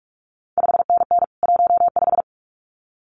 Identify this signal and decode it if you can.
Morse code, heard as a keyed tone, about 46 wpm, 700 Hz. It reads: 5NN 15
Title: CQ WW DX Contest Exchange Encore - 45wpm